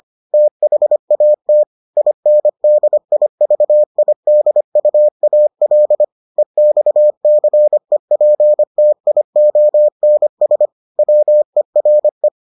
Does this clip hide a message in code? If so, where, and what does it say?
Yes: Morse code from 0.3 to 12.3 s, heard as a keyed tone, about 25 wpm, 600 Hz. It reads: THAT INDIVIDUAL EXCEPTIONS WERE